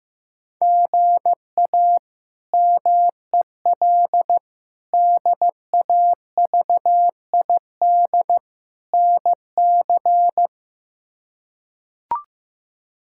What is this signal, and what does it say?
Morse code, heard as a keyed tone, about 15 wpm, 700 Hz. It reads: GA MEL DAVID NC
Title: K1USN SST QSO Practice - 15wpm